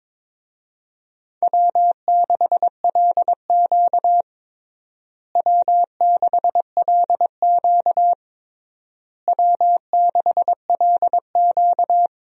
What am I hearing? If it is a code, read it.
Morse code, heard as a keyed tone, about 22 wpm, 700 Hz. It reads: W6LQ W6LQ W6LQ
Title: USA Call Signs - 4 Characters - Repeated 3x - 22wpm